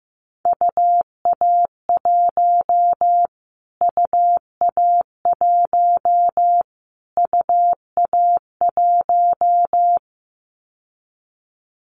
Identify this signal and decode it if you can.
Morse code, heard as a keyed tone, about 15 wpm, 700 Hz. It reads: UA1 UA1 UA1
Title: Callsign Prefixes for Top 25 DXCC - Repeated 3x - 15wpm